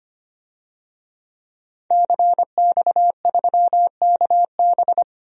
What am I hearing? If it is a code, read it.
Morse code, heard as a keyed tone, about 25 wpm, 700 Hz. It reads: CX3KB
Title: DXCC-100 Call Signs - 5 Characters - 25wpm